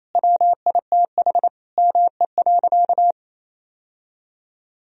Morse code, heard as a keyed tone, about 28 wpm, 700 Hz. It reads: WITH ME.
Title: Sentences from Top 1000 Words - Review Part 2 - 28wpm